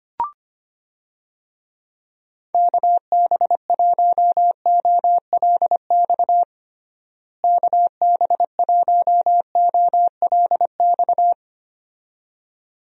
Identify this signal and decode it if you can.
Morse code, heard as a keyed tone, about 25 wpm, 700 Hz. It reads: KB1OLX KB1OLX
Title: USA Call Signs - 6 Characters - Repeated 2x - 25wpm